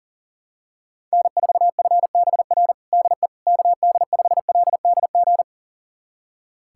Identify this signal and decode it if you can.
Morse code, heard as a keyed tone, about 37 wpm, 700 Hz. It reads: N4FBR DE KD5LDG
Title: USA Callsign DE Callsign 40wpm